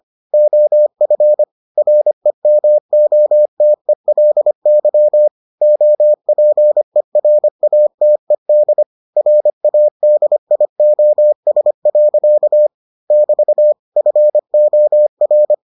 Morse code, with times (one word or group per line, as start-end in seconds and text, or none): 0.3-1.4 OF
1.8-5.3 REMOTELY
5.6-8.8 OPERATED
9.2-12.7 RADIOS.
13.1-13.7 <BT>
14.0-15.5 FOR